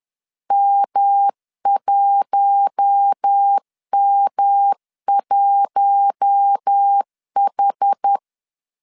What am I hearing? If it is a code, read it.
Morse code, heard as a keyed tone, about 10 wpm, 800 Hz. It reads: M1M1H